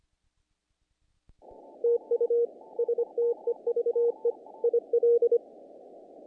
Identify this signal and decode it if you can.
Morse code, heard as a keyed tone, about 26 wpm, 470 Hz. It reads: TU STEVE IL
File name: contests / naqp-cw-k5tr-op-k5pi- -1800